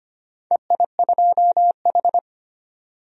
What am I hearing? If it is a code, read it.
Morse code, heard as a keyed tone, about 25 wpm, 700 Hz. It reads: EI2H
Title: DXCC-100 Call Signs - 4 Characters - 25wpm